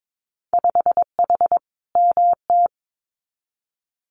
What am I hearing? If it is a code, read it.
Morse code, heard as a keyed tone, about 22 wpm, 700 Hz. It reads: 5H MT
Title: Winter Field Day Exchanges Encore 22wpm